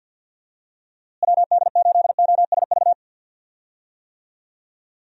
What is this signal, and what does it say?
Morse code, heard as a keyed tone, about 50 wpm, 700 Hz. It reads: WD8OSV
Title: USA Call Signs - 6 Characters - Repeated 3x - 50wpm